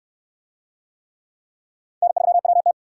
Morse code, heard as a keyed tone, about 69 wpm, 700 Hz. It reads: N3WT